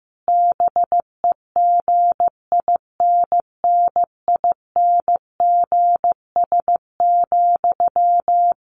Morse code, heard as a keyed tone, about 15 wpm, 700 Hz. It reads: BEGINNINGS,
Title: A Sound Of Thunder 15wpm